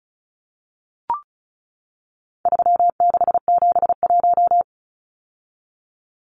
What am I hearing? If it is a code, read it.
Morse code, heard as a keyed tone, about 34 wpm, 700 Hz. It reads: 3671
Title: Numbers - 4-Digit 35wpm